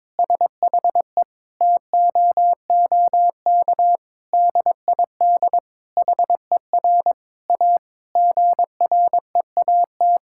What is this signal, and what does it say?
Morse code, heard as a keyed tone, about 22 wpm, 700 Hz. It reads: SHE TOOK DID HER A GREAT
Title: Sentences from Top 1000 Words - Review Part 6 - 22wpm